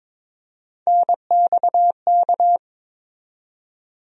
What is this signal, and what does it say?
Morse code, heard as a keyed tone, about 22 wpm, 700 Hz. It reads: NXK
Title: USA Call Sign Postfixes - 3 Letters 22wpm